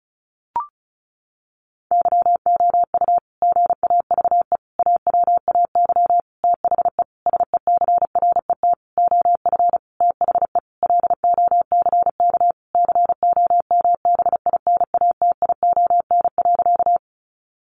Morse code, heard as a keyed tone, about 34 wpm, 700 Hz. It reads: YOU GAVE AWAY THE SECRET OF THE LOCK COMBINATION.
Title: Sentences from Top 1300 Words 35wpm